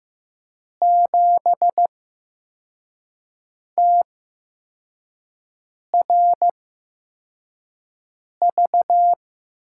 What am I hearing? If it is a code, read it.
Morse code, heard as a keyed tone, about 15 wpm, spaced as for 4 wpm, 700 Hz. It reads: 7TRV
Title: USA Call Signs - 6 Characters - 8x Character Spacing - 15wpm